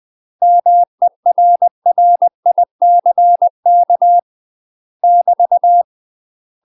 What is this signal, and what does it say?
Morse code, heard as a keyed tone, about 20 wpm, 700 Hz. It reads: MERRICK <BT>
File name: UK MASTER QSOs TRAINING / 20WPM / UK MASTER QSOs 0063